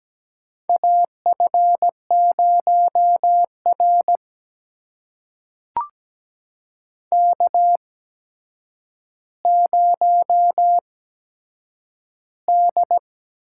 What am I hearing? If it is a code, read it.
Morse code, heard as a keyed tone, about 17 wpm, spaced as for 8 wpm, 700 Hz. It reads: AF0R K0D
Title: USA Call Signs - 4 Characters - 8x Character Spacing - 17wpm